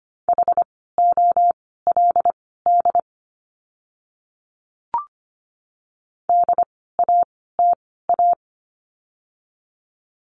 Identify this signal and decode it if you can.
Morse code, heard as a keyed tone, about 25 wpm, spaced as for 16 wpm, 700 Hz. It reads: HOLD DATA
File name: course / 25:10wpm / 094 Lesson 16f - Letter c - Words 25wpm